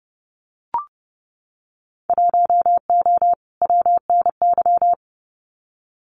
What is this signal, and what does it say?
Morse code, heard as a keyed tone, about 30 wpm, 700 Hz. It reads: 1O WNY
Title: Winter Field Day Exchanges 30wpm